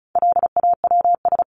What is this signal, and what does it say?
Morse code, heard as a keyed tone, about 34 wpm, 700 Hz. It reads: LAWS
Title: ES:S Suffix - Top 100 Words with Base Words 35wpm